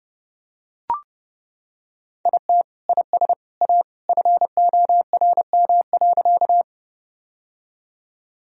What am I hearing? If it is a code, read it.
Morse code, heard as a keyed tone, about 30 wpm, 700 Hz. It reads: IT IS A FORM.
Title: Sentences from Top 1000 Words - Review Part 1 - 30wpm